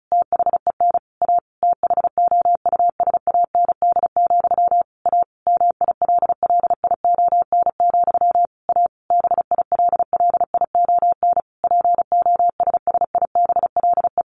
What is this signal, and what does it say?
Morse code, heard as a keyed tone, about 34 wpm, 700 Hz. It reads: THEN A THOUSAND, A MILLION, A BILLION POSSIBLE
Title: A Sound Of Thunder 35wpm